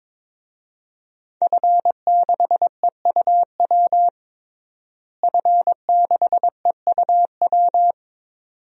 Morse code, heard as a keyed tone, about 22 wpm, 700 Hz. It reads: F6EUW F6EUW
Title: DXCC-100 Call Signs - 5 Characters - Repeated 2x - 22wpm